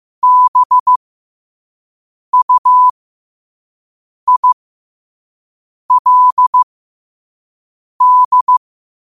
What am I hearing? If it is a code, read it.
Morse code, heard as a keyed tone, about 15 wpm, spaced as for 5 wpm, 1000 Hz. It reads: BUILD
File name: qotdfiles / 5wpm